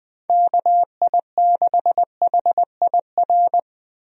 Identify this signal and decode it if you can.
Morse code, heard as a keyed tone, about 20 wpm, 700 Hz. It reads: KI6HIR
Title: USA Call Signs - 6 Characters - Repeated 3x - 20wpm